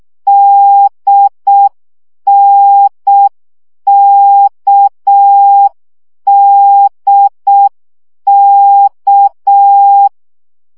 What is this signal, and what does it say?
Morse code, heard as a keyed tone, about 6 wpm, 800 Hz. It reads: DNKDK
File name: CW gyakorlas n d k 5 baud